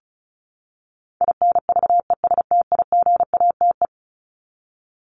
Morse code, heard as a keyed tone, about 34 wpm, 700 Hz. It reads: INVESTIGATE
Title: Top 1751-2000 Words 35wpm